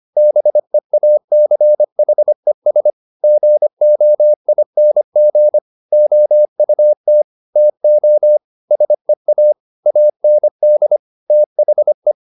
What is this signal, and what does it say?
Morse code, heard as a keyed tone, about 25 wpm, 600 Hz. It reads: BEACHES GOING OUT TO SEA AND THE
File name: the old man and the sea part 1 25wpm